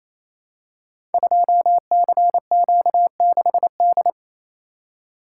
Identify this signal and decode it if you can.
Morse code, heard as a keyed tone, about 28 wpm, 700 Hz. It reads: 2CQ6D
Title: Code Groups - 5 Letter:Numbers - 28wpm